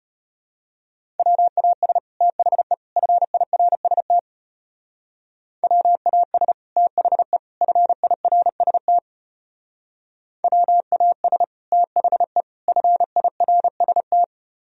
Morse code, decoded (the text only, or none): WAS THE FIRST WAS THE FIRST WAS THE FIRST